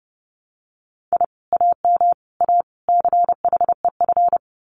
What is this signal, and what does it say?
Morse code, heard as a keyed tone, about 30 wpm, 700 Hz. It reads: I AM A CHEF
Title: Common QSO Phrases Encore 30wpm